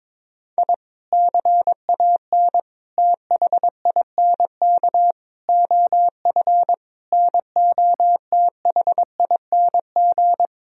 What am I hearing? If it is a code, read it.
Morse code, heard as a keyed tone, about 22 wpm, 700 Hz. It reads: I CAN THINK OF NOTHING
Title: Sentences from Top 1000 Words - Review Part 1 - 22wpm